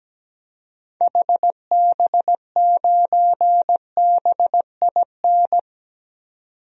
Morse code, heard as a keyed tone, about 17 wpm, 700 Hz. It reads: HB9BIN
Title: DXCC-100 Call Signs - 6 Characters - 17wpm